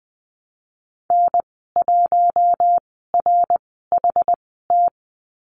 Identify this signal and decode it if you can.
Morse code, heard as a keyed tone, about 19 wpm, spaced as for 14 wpm, 700 Hz. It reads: N1RHT
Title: 062 Lesson 11g - Letter h - Call Signs 20wpm